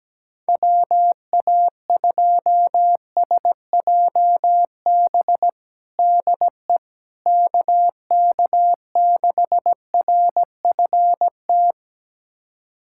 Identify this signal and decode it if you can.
Morse code, heard as a keyed tone, about 17 wpm, 700 Hz. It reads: WA2SJB DE KK6RFT
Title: USA Callsign DE Callsign - 6 Characters - 17wpm